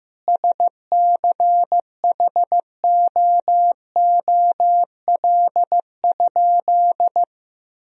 Morse code, heard as a keyed tone, about 16 wpm, 700 Hz. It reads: SCHOOL?
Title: Sentences from Top 1000 Words - Review Part 1 - 15wpm